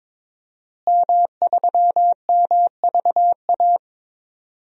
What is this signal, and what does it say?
Morse code, heard as a keyed tone, about 22 wpm, 700 Hz. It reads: M3MVA
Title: DXCC-100 Call Signs - 5 Characters - Repeated 2x - 22wpm